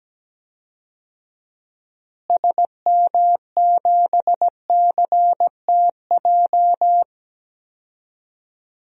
Morse code, heard as a keyed tone, about 17 wpm, 700 Hz. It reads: SM7CTJ
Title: DXCC-100 Call Signs - 6 Characters - Repeated 2x - 17wpm